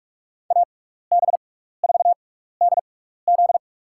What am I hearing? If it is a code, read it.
Morse code, heard as a keyed tone, about 47 wpm, spaced as for 16 wpm, 700 Hz. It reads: AB4DZ
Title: USA Call Signs - 6x Character Spacing - 45wpm